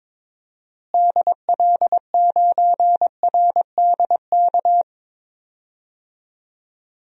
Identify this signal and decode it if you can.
Morse code, heard as a keyed tone, about 21 wpm, 700 Hz. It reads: DL9RDK
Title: DXCC-100 Call Signs - 6 Characters - Repeated 3x - 22wpm